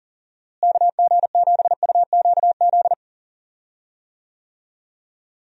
Morse code, heard as a keyed tone, about 40 wpm, 700 Hz. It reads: KG7UQZ
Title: USA Call Signs - 6 Characters - Repeated 2x - 40wpm